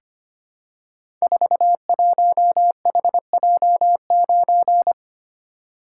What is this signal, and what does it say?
Morse code, heard as a keyed tone, about 25 wpm, 700 Hz. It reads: 41HJ9
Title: Code Groups - 5 Letter:Numbers - 25wpm